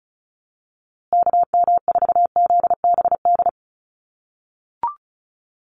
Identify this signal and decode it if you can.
Morse code, heard as a keyed tone, about 34 wpm, 700 Hz. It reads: KM4ZBD
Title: USA Call Signs Repeated 3x 35wpm